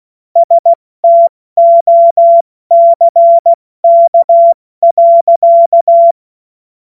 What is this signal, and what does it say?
Morse code, heard as a keyed tone, about 16 wpm, 680 Hz. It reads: STOCK.
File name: Morsekurs / Witze / Witz 12 März 2024 70 F80